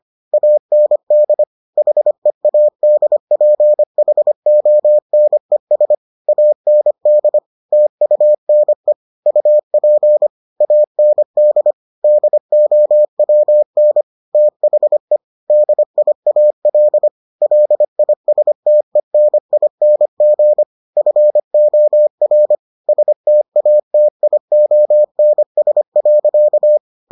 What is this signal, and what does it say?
Morse code, heard as a keyed tone, about 25 wpm, 600 Hz. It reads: AND HEADPHONES AND TUNE UP AND DOWN THE DIAL LISTENING FOR STATIONS.